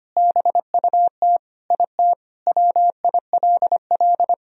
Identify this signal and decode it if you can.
Morse code, heard as a keyed tone, about 25 wpm, 700 Hz. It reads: BUT IT WILL
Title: Sentences from Top 1000 Words - Review Part 2 - 25wpm